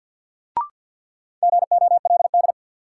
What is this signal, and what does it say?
Morse code, heard as a keyed tone, about 53 wpm, 700 Hz. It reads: GOLD
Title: Sentences from Top 1000 Words - Review Part 3 - New Word Drill - 50wpm